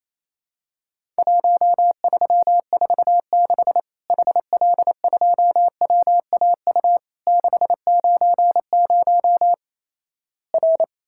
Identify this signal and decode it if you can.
Morse code, heard as a keyed tone, about 28 wpm, 700 Hz. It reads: 1346 HL2WAU 690 R
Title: Worked All Europe DX Contest - QTC Exchange - 28wpm